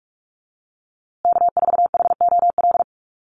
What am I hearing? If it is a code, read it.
Morse code, heard as a keyed tone, about 44 wpm, 700 Hz. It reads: K4HOL
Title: USA Call Signs Repeated 3x 45wpm